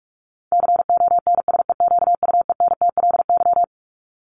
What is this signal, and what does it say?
Morse code, heard as a keyed tone, about 44 wpm, 700 Hz. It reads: CONSEQUENTLY
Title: LY Suffix - Top 100 Words with Base Words 45wpm